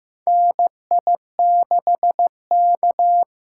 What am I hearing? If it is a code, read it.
Morse code, heard as a keyed tone, about 15 wpm, 700 Hz. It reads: NI6K
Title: USA Call Signs - 4 Characters - 8x Character Spacing - 15wpm